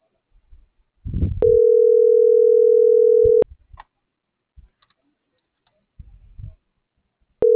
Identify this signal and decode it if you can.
Ringback tone (North America), 440 + 480 Hz.
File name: CENTRAL MABE / out celeste.alcantara